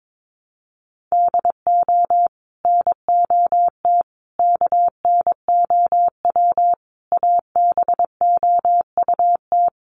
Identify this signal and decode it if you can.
Morse code, heard as a keyed tone, about 22 wpm, 700 Hz. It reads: DO NOT KNOW ABOUT
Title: Sets of 5 Words Encore2 22wpm